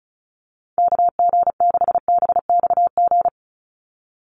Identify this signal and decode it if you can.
Morse code, heard as a keyed tone, about 36 wpm, 700 Hz. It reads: KG6BXG
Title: USA Call Signs 35wpm